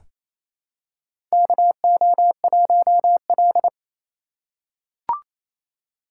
Morse code, heard as a keyed tone, about 28 wpm, 700 Hz. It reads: KO1L
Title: USA Call Signs - 4 Characters - 4x Character Spacing - 28wpm